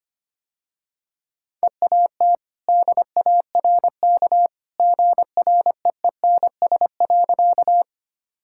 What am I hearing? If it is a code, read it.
Morse code, heard as a keyed tone, about 25 wpm, 700 Hz. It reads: EAT DARK GREENS.
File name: Easy Sentences with Hints - 25wpm